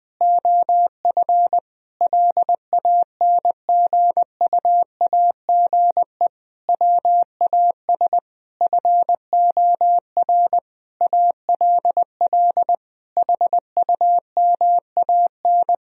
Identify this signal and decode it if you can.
Morse code, heard as a keyed tone, about 20 wpm, 700 Hz. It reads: OF LANGUAGE WAS FOR ALL HUMAN
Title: Sentences from Top 1000 Words - Review Part 4 - 20wpm